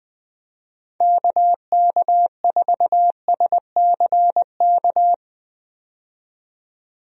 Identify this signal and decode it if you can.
Morse code, heard as a keyed tone, about 20 wpm, 700 Hz. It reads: KK4SCK
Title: USA Call Signs - 6 Characters - 20wpm